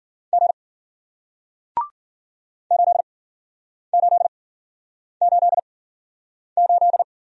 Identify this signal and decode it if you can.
Morse code, heard as a keyed tone, about 54 wpm, spaced as for 9 wpm, 700 Hz. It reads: G 8888